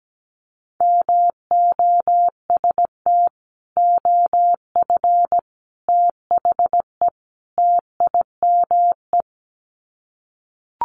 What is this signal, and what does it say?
Morse code, heard as a keyed tone, about 17 wpm, 700 Hz. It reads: MOST OF THE TIME
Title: Sets of 4 Words 17wpm